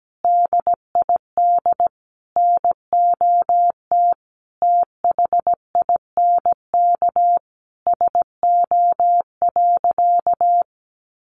Morse code, heard as a keyed tone, about 17 wpm, 700 Hz. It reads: DID NOT THINK SO.
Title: Sentences from Top 1500 Words 17wpm